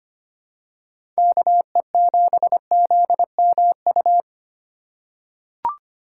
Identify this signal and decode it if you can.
Morse code, heard as a keyed tone, about 25 wpm, 700 Hz. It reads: KE7ZMU
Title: USA Call Signs - 6 Characters - 6x Character Spacing - 25wpm